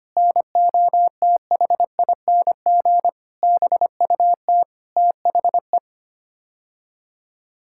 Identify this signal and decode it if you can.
Morse code, heard as a keyed tone, about 25 wpm, 700 Hz. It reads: NOTHING BUT THE
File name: Sets of 9 Words - 25wpm